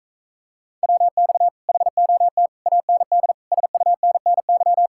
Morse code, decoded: WX HOT AND SUNNY

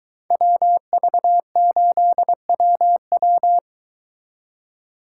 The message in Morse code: WV8WW

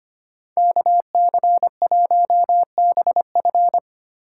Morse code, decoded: KC1BF

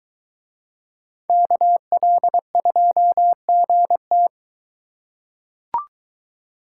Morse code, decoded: KL2GT

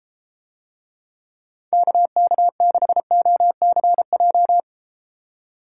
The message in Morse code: KK6OCJ

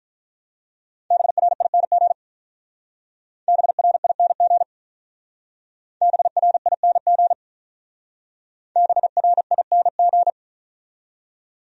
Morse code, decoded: BRING BRING BRING BRING

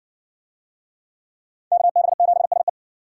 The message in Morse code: KB7SE

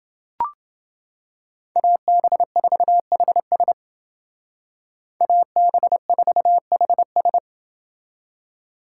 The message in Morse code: AB4HS AB4HS

